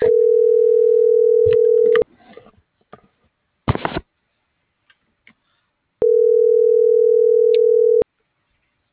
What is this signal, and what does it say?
Ringback tone (North America), 440 + 480 Hz.